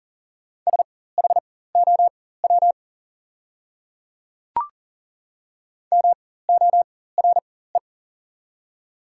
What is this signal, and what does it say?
Morse code, heard as a keyed tone, about 40 wpm, spaced as for 19 wpm, 700 Hz. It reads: SHOW MORE
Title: 4x Character Spacing - Top 200 Words - 40wpm